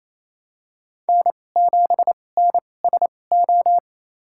Morse code, heard as a keyed tone, about 27 wpm, spaced as for 20 wpm, 700 Hz. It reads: N7NSO
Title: USA Call Signs - 5 Characters - 2x Character Spacing - 28wpm